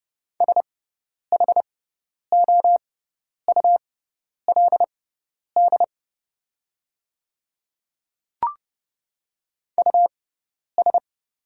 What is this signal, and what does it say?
Morse code, heard as a keyed tone, about 30 wpm, spaced as for 10 wpm, 700 Hz. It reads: SHOULD US
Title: 6x Character Spacing - Top 300 Words - 30wpm